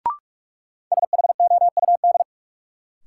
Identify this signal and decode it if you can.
Morse code, heard as a keyed tone, about 46 wpm, 700 Hz. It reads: SHOUD
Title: Sentences from Top 1000 Words - Review Part 4 - New Word Drill - 45wpm